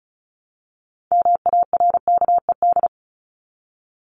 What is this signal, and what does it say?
Morse code, heard as a keyed tone, about 34 wpm, 700 Hz. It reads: MARKED
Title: ED Suffix - Top 100 Words with Base Words 35wpm